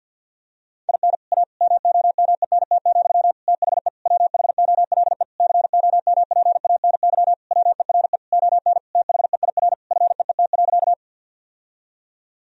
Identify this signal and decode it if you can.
Morse code, heard as a keyed tone, about 53 wpm, 700 Hz. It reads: IN A MOMENT, THE WHOLE COMPANY WERE ON THEIR FEET.
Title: Sentences from Top 1000 Words - Review Part 3 - 50wpm